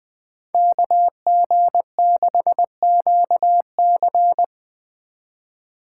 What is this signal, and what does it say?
Morse code, heard as a keyed tone, about 20 wpm, 700 Hz. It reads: KG6QC
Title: Speed Racing - USA 2x2 Call Signs 17wpm